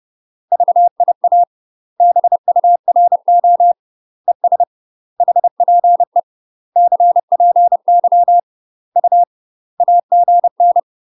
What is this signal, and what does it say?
Morse code, heard as a keyed tone, about 30 wpm, 700 Hz. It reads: VIA BURO ES HPE CPY U AGN
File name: UK MASTER QSOs TRAINING / 30WPM / UK MASTER QSOs 0049